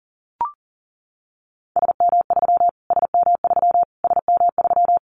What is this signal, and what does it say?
Morse code, heard as a keyed tone, about 40 wpm, 700 Hz. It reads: SM3 SM3 SM3
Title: Callsign Prefixes for Top 25 DXCC - Repeated 3x - 40wpm